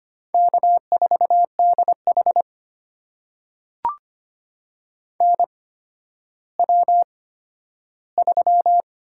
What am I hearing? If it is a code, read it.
Morse code, heard as a keyed tone, about 25 wpm, spaced as for 12 wpm, 700 Hz. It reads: K4DH NW3
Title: USA Call Signs - 4 Characters - 8x Character Spacing - 25wpm